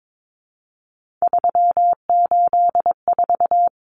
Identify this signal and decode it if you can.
Morse code, heard as a keyed tone, about 22 wpm, 700 Hz. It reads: 384
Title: Common Cut Numbers - 3-Digit 22wpm